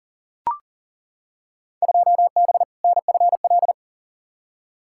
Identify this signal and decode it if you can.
Morse code, heard as a keyed tone, about 37 wpm, 700 Hz. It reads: 2B NFL
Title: Field Day Exchanges Encore - 40wpm